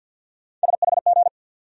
Morse code, heard as a keyed tone, about 49 wpm, 700 Hz. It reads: SHG
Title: USA Call Sign Postfixes - 3 Letters 50wpm